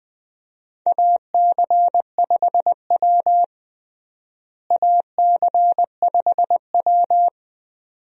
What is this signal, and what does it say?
Morse code, heard as a keyed tone, about 20 wpm, 700 Hz. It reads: AC5W AC5W
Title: USA Call Signs - 4 Characters - Repeated 2x - 20wpm